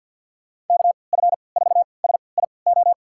Morse code, heard as a keyed tone, about 53 wpm, spaced as for 29 wpm, 700 Hz. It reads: KF4SIO
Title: USA Call Signs - 6 Characters - 3x Character Spacing - 50wpm